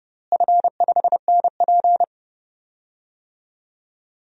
Morse code, heard as a keyed tone, about 30 wpm, 700 Hz. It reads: F5NP
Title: DXCC-100 Call Signs - 4 Characters - Repeated 3x - 30wpm